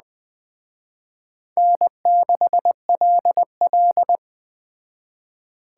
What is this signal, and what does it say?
Morse code, heard as a keyed tone, about 20 wpm, 700 Hz. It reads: N6LL
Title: Speed Racing - USA 1x2 Call Signs 17wpm